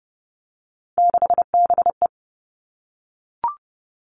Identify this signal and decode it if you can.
Morse code, heard as a keyed tone, about 30 wpm, 700 Hz. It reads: 6BE
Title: Three Challenging Letter-Number Combinations 30wpm